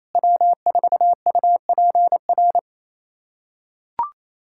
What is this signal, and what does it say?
Morse code, heard as a keyed tone, about 28 wpm, 700 Hz. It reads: W4UPR
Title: USA Callsign DE Callsign 28wpm